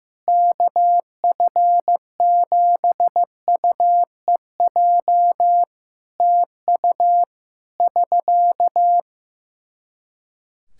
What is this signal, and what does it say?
Morse code, heard as a keyed tone, about 15 wpm, 700 Hz. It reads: KF7UEJ TU <SK>
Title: Practice QSO - Exchange 4 - Ending - 15wpm